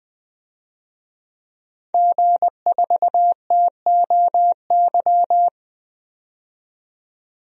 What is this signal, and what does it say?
Morse code, heard as a keyed tone, about 20 wpm, 700 Hz. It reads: G4TOY
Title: DXCC-100 Call Signs - 5 Characters - 20wpm